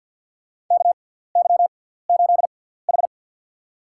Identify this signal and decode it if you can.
Morse code, heard as a keyed tone, about 49 wpm, spaced as for 17 wpm, 700 Hz. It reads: KY8H
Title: USA Call Signs - 4 Characters - 6x Character Spacing - 50wpm